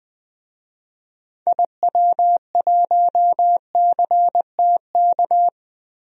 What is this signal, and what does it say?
Morse code, heard as a keyed tone, about 20 wpm, 700 Hz. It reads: IW1CTK
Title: DXCC-100 Call Signs - 6 Characters - 20wpm